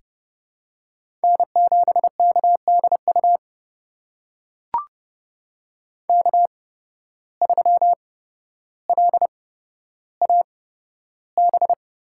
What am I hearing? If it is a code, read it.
Morse code, heard as a keyed tone, about 30 wpm, spaced as for 13 wpm, 700 Hz. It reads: N7KDU K3LAB
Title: USA Call Signs - 5 Characters - 8x Character Spacing - 30wpm